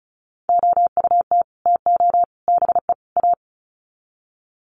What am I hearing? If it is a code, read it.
Morse code, heard as a keyed tone, about 34 wpm, 700 Hz. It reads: OUT TO BE A
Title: Sets of 4 Words 35wpm